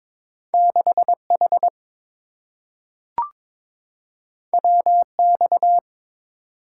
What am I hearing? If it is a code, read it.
Morse code, heard as a keyed tone, about 22 wpm, 700 Hz. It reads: 6H WX
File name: practice / 22wpm / Two Letter-Number Combinations - 22wpm